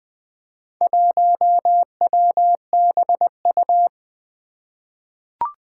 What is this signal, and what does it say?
Morse code, heard as a keyed tone, about 20 wpm, 700 Hz. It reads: 1WBU
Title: Four Challenging Letter-Number Combinations 20wpm